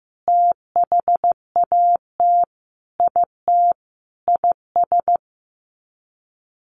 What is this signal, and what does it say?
Morse code, heard as a keyed tone, about 15 wpm, 700 Hz. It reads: THAT IT IS
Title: Sets of 5 Words Encore2 15wpm